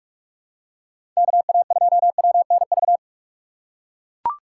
Morse code, heard as a keyed tone, about 43 wpm, 700 Hz. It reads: K A 2 W N V
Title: USA Call Signs - 6 Characters - 6x Character Spacing - 45wpm